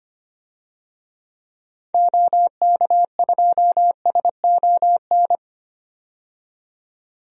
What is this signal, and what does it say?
Morse code, heard as a keyed tone, about 25 wpm, 700 Hz. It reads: OK2SON
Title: DXCC-100 Call Signs - 6 Characters - 25wpm